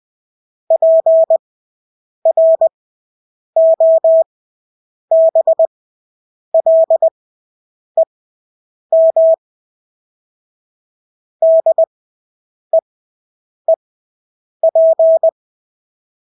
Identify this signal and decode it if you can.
Morse code, heard as a keyed tone, about 20 wpm, spaced as for 8 wpm, 650 Hz. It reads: PROBLEM DEEP
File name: Engelske ord 20/8 wpm